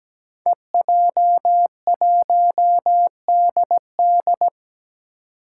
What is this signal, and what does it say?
Morse code, heard as a keyed tone, about 17 wpm, 700 Hz. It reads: EJ1DD
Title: Five Challenging Letter-Number Combinations 17wpm